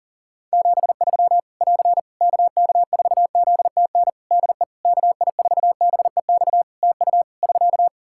Morse code, heard as a keyed tone, about 37 wpm, 700 Hz. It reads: 73 <AR> KK4ZTN DE KI4BEX TU <SK>
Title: K1USN SST QSO Practice 40wpm